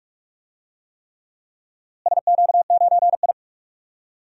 Morse code, heard as a keyed tone, about 46 wpm, 700 Hz. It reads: SQ9I
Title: DXCC-100 Call Signs - 4 Characters - Repeated 3x - 45wpm